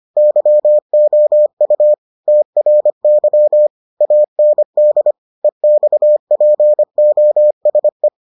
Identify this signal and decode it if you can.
Morse code, heard as a keyed tone, about 25 wpm, 600 Hz. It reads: YOU TRY AND EXPOSE